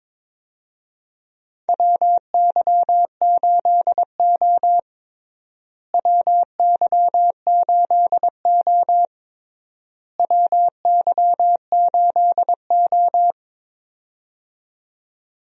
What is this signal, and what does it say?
Morse code, heard as a keyed tone, about 22 wpm, 700 Hz. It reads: WY8O WY8O WY8O
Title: USA Call Signs - 4 Characters - Repeated 3x - 22wpm